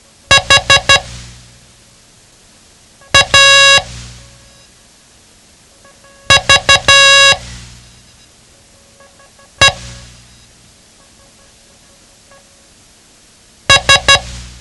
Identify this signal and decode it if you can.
Morse code, heard as a keyed tone, about 12 wpm, 1650 Hz. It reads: H A V E S
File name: ham radio / CodeQuick / Code Quick 07